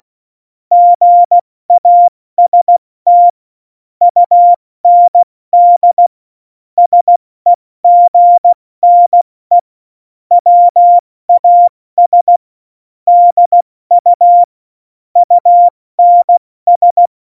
Morse code, heard as a keyed tone, about 16 wpm, 700 Hz. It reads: GAST UND SEGNE WAS DU UNS
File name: Morsekurs / Witze / Witz 09. Oktober 2018, 70 F80